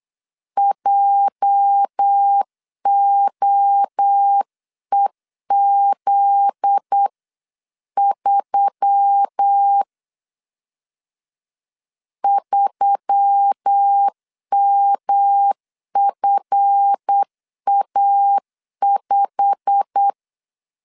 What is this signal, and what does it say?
Morse code, heard as a keyed tone, about 8 wpm, 800 Hz. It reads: JOEZ3 3MFA5
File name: LEZIONI / 8 YJ45 / 8 YJ45 8wpm 3214